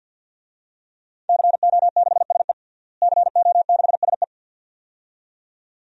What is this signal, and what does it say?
Morse code, heard as a keyed tone, about 52 wpm, 700 Hz. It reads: CO6SE CO6SE
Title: DXCC-100 Call Signs - 5 Characters - Repeated 2x - 50wpm